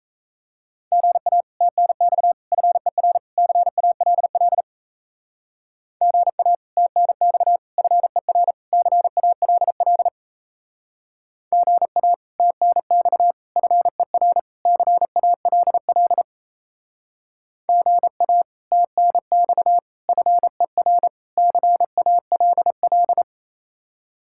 Morse code, heard as a keyed tone, about 33 wpm, 700 Hz. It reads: GA TNX FER CALL GA TNX FER CALL GA TNX FER CALL GA TNX FER CALL
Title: Speed-Racing Common QSO Phrases - 28wpm